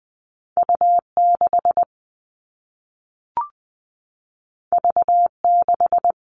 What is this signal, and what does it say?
Morse code, heard as a keyed tone, about 20 wpm, 700 Hz. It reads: U6 V6
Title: All Cut Numbers - 2-Digit 20wpm